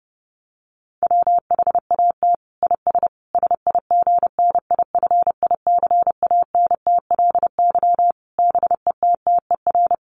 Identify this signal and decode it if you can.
Morse code, heard as a keyed tone, about 28 wpm, 700 Hz. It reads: WHAT IS SIGNIFICANTLY BETTER
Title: Sentences from Top 1500 Words 30wpm